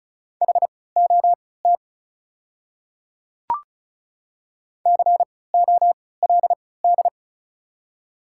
Morse code, heard as a keyed tone, about 36 wpm, spaced as for 20 wpm, 700 Hz. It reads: HOT COLD
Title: 3x Character Spacing - Top 500 Words - 35wpm